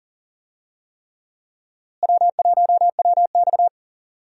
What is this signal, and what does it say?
Morse code, heard as a keyed tone, about 40 wpm, 700 Hz. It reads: W1WX
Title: USA Call Signs - 4 Characters - Repeated 3x - 40wpm